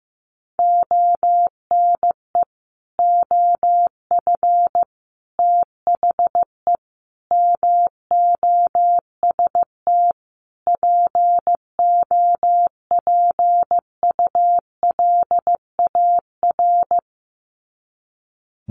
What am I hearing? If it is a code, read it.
Morse code, heard as a keyed tone, about 15 wpm, 700 Hz. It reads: ONE OF THE MOST POPULAR
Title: Sets of 5 Words 15wpm